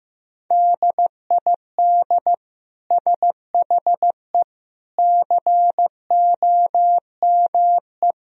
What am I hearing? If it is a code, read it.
Morse code, heard as a keyed tone, about 15 wpm, 700 Hz. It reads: DID SHE COME
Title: Sentences from Top 1000 Words - Review Part 2 - 15wpm